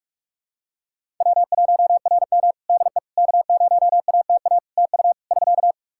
Morse code, heard as a keyed tone, about 46 wpm, 700 Hz. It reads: W1RM DE K0ATA TU <SK>
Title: Practice QSO - Exchange 4 - Ending - 45wpm